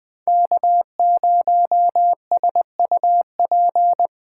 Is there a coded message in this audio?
Yes: Morse code from 0.3 to 4.1 s, heard as a keyed tone, about 21 wpm, 700 Hz. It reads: K0SUP